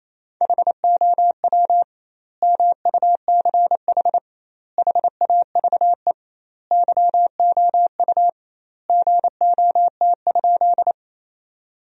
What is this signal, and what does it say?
Morse code, heard as a keyed tone, about 28 wpm, 700 Hz. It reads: HOW MUCH HAVE YOU GOT?
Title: 2x Word-Spacing - Sentences from Top 300 Words - 28wpm